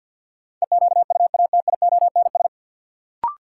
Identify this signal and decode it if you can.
Morse code, heard as a keyed tone, about 46 wpm, 700 Hz. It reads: EQUATIONS
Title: Sets of 5 Words - New Word Drill - 50wpm